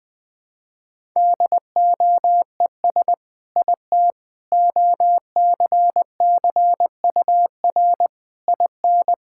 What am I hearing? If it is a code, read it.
Morse code, heard as a keyed tone, about 20 wpm, 700 Hz. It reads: DOES IT OCCUR IN
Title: Sentences from Top 1000 Words - Review Part 6 - 20wpm